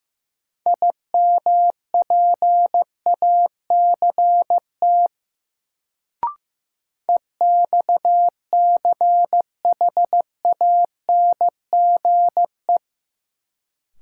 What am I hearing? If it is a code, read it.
Morse code, heard as a keyed tone, about 15 wpm, 700 Hz. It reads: IMPACT EXCHANGE
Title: Sentences from Top 1000 Words - Review Part 6 - New Word Drill - 15wpm